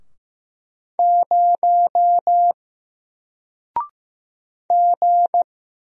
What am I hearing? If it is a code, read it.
Morse code, heard as a keyed tone, about 15 wpm, 700 Hz. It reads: 0 G
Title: Single Character (Letter:Number:Punctuation:Prosign) 15wpm